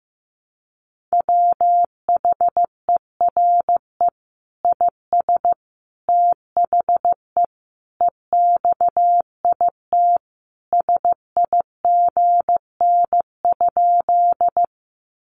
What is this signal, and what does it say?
Morse code, heard as a keyed tone, about 15 wpm, 700 Hz. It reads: WHERE IS THE EXIT SIGN?
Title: Sentences from Top 1100 Words 15wpm